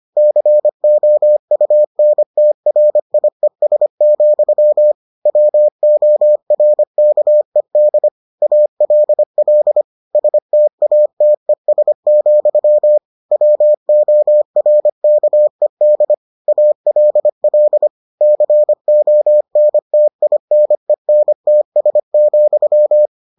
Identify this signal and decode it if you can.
Morse code, heard as a keyed tone, about 25 wpm, 600 Hz. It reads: COUNTRIES, WORKED ALL STATES, WORKED ALL CONTINENTS,